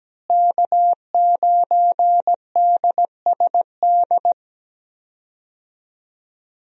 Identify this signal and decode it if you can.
Morse code, heard as a keyed tone, about 17 wpm, 700 Hz. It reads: K9DSD
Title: USA Call Signs - 5 Characters - Repeated 3x - 17wpm